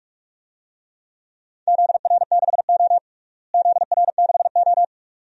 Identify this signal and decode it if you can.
Morse code, heard as a keyed tone, about 46 wpm, 700 Hz. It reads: ZR6O ZR6O
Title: DXCC-100 Call Signs - 4 Characters - Repeated 2x - 45wpm